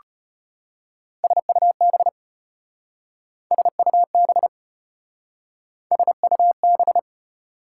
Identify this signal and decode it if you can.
Morse code, heard as a keyed tone, about 33 wpm, 700 Hz. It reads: SUB SUB SUB